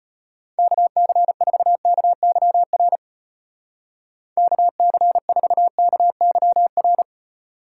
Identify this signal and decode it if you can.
Morse code, heard as a keyed tone, about 32 wpm, 700 Hz. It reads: KC4KYR KC4KYR